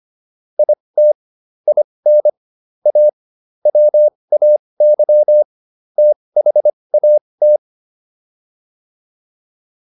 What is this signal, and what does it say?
Morse code, heard as a keyed tone, about 25 wpm, spaced as for 20 wpm, 600 Hz. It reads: IT IN A WAY THAT